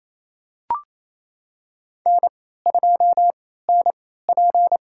Morse code, heard as a keyed tone, about 28 wpm, spaced as for 16 wpm, 700 Hz. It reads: N2NP
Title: USA Call Signs - 4 Characters - 3x Character Spacing - 28wpm